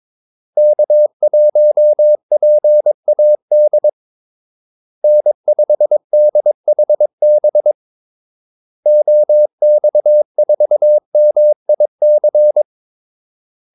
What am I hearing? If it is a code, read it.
Morse code, heard as a keyed tone, about 22 wpm, 600 Hz. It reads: K1PAD N5DHB OX4MIC
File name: Callsigns-22wpm